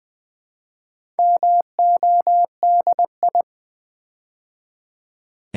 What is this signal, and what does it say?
Morse code, heard as a keyed tone, about 20 wpm, 700 Hz. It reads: MODI
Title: Build A Word - 4 Letters At A Time - 12-Letter Words - 20wpm